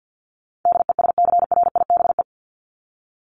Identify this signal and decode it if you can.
Morse code, heard as a keyed tone, about 50 wpm, 700 Hz. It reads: DESCRIBE
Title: ED Suffix - Top 100 Words with Base Words 50wpm